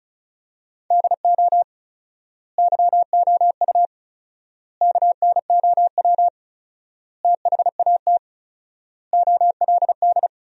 Morse code, heard as a keyed tone, about 36 wpm, 700 Hz. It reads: DO YOU KNOW THAT OLD
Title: 4x Word-Spacing - Sentences from Top 400 Words - 35wpm